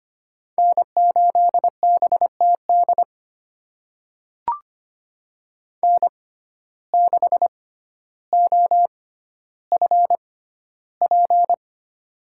Morse code, heard as a keyed tone, about 25 wpm, spaced as for 13 wpm, 700 Hz. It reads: N8BTD N6OFP
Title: USA Call Signs - 5 Characters - 6x Character Spacing - 25wpm